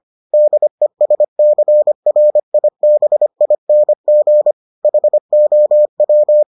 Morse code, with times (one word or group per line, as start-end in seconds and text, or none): 0.3-4.5 DESCRIBING
4.8-6.4 HOW